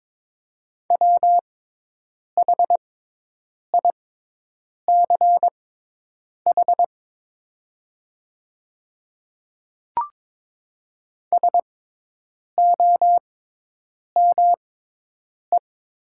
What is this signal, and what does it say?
Morse code, heard as a keyed tone, about 22 wpm, spaced as for 8 wpm, 700 Hz. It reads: WHICH SOME
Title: 6x Character Spacing - Top 100 Words - 22wpm